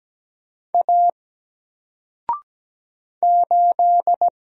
Morse code, heard as a keyed tone, about 17 wpm, 700 Hz. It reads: A 8
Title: Single Letter-Number 17wpm